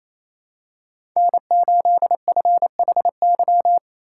Morse code, heard as a keyed tone, about 28 wpm, 700 Hz. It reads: N8FHY
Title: USA Call Signs - 5 Characters - 4x Character Spacing - 28wpm